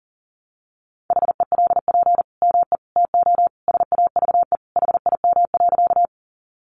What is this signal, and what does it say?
Morse code, heard as a keyed tone, about 40 wpm, 700 Hz. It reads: HELP ME TO SAVE HIM.
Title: Sentences from Top 1100 Words 40wpm